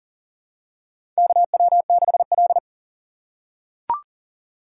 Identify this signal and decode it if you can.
Morse code, heard as a keyed tone, about 37 wpm, 700 Hz. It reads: KW6L
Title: USA Call Signs - 4 Characters - 3x Character Spacing - 40wpm